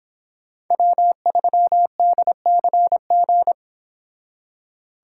Morse code, heard as a keyed tone, about 26 wpm, 700 Hz. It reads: W3DCG